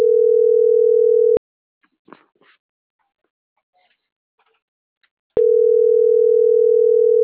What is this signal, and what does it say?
Ringback tone (North America), 440 + 480 Hz.